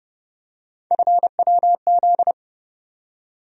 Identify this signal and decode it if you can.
Morse code, heard as a keyed tone, about 32 wpm, 700 Hz. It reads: FWZ
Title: USA Call Sign Postfixes - 3 Letters 30wpm